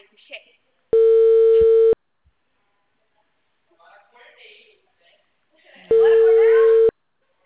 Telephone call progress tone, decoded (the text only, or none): ringback tone